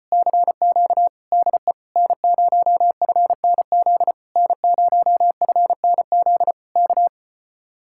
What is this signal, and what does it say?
Morse code, heard as a keyed tone, about 37 wpm, 700 Hz. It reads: CQ DE N0FNZ N0FNZ K